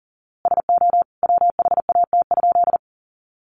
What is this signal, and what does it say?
Morse code, heard as a keyed tone, about 40 wpm, 700 Hz. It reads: SO WHAT?
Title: A Sound Of Thunder 40wpm